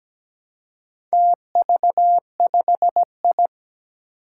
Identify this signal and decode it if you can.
Morse code, heard as a keyed tone, about 17 wpm, 700 Hz. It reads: TV5I
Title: Four Challenging Letter-Number Combinations 17wpm